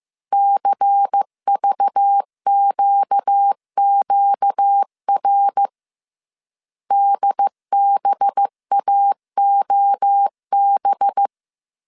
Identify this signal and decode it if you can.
Morse code, heard as a keyed tone, about 15 wpm, 800 Hz. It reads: CVQQR DBAOB